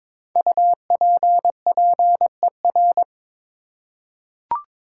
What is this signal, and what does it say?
Morse code, heard as a keyed tone, about 23 wpm, 700 Hz. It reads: UPPER
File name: 5 Letter Words - First 3 Letters Given - 22wpm